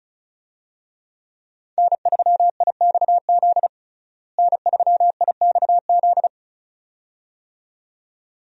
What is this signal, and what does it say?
Morse code, heard as a keyed tone, about 36 wpm, 700 Hz. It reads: N3IXZ N3IXZ
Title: USA Call Signs - 5 Characters - Repeated 2x - 35wpm